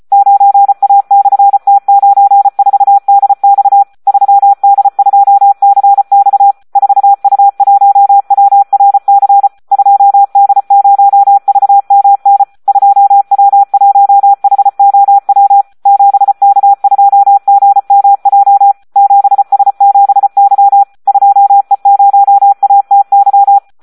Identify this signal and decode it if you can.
Morse code, heard as a keyed tone, about 34 wpm, 800 Hz, with radio noise. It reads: 9A/T94DX 3D2CX 4U1WRC 2D0VMN 2W1HOW 7K2GMJ 7S7Y 2E0ATY
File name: Audio practica CW / SD 1 34